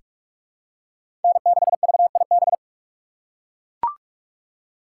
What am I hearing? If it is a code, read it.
Morse code, heard as a keyed tone, about 43 wpm, 700 Hz. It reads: N6VIB
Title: USA Call Signs - 5 Characters - 4x Character Spacing - 45wpm